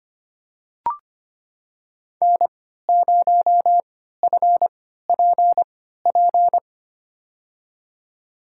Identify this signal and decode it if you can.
Morse code, heard as a keyed tone, about 25 wpm, spaced as for 14 wpm, 700 Hz. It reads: N0FPP
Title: USA Call Signs - 3x Character Spacing - 25wpm